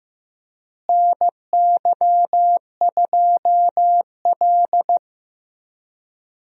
Morse code, heard as a keyed tone, about 15 wpm, 700 Hz. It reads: NY2L
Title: USA Call Signs - 4 Characters - Repeated 3x - 15wpm